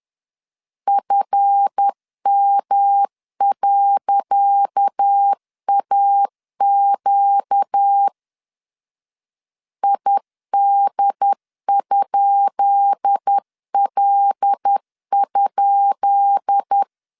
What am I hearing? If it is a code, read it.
Morse code, heard as a keyed tone, about 10 wpm, 800 Hz. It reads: FM.AQ ID?L?
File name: LEZIONI / 10 SIGNS / 10C DOT 10wpm 2940